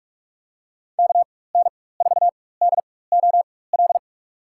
Morse code, heard as a keyed tone, about 46 wpm, spaced as for 21 wpm, 700 Hz. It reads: KN4DOL
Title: USA Call Signs - 6 Characters - 4x Character Spacing - 45wpm